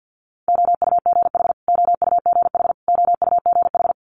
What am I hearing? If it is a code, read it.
Morse code, heard as a keyed tone, about 50 wpm, 700 Hz. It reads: OUGH OUGH OUGH
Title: Multi-letter Phonemes Repeated 3x 50wpm